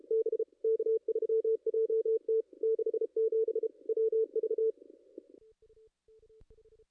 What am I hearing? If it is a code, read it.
Morse code, heard as a keyed tone, about 38 wpm, 450 Hz. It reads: B K3JT 67 WV
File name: contests / ss-cw-k5tr-op-n5rz -2300